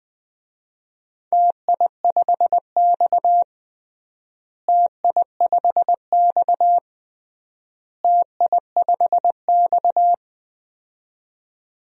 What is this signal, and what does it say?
Morse code, heard as a keyed tone, about 21 wpm, 700 Hz. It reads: TI5X TI5X TI5X
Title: DXCC-100 Call Signs - 4 Characters - Repeated 3x - 20wpm